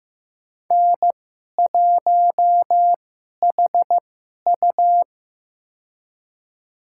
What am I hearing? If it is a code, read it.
Morse code, heard as a keyed tone, about 15 wpm, spaced as for 11 wpm, 700 Hz. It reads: N1HU
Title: USA Call Signs - 2x Character Spacing - 15wpm